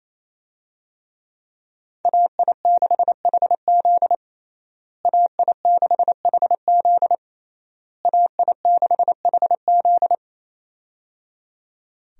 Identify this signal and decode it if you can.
Morse code, heard as a keyed tone, about 28 wpm, 700 Hz. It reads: AI6HZ AI6HZ AI6HZ
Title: USA Call Signs - 5 Characters - Repeated 3x - 28wpm